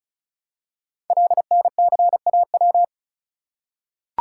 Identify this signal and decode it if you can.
Morse code, heard as a keyed tone, about 36 wpm, 700 Hz. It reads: LNCAW
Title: Code Groups - 5 Letters - 35wpm